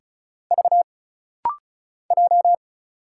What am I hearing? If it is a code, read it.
Morse code, heard as a keyed tone, about 35 wpm, 700 Hz. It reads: V J
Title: Single Letters - Rapid-Fire 35wpm